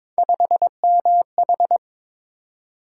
Morse code, heard as a keyed tone, about 21 wpm, 700 Hz. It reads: 5MH
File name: practice / 22wpm / Three Letter-Number Combinations - 22wpm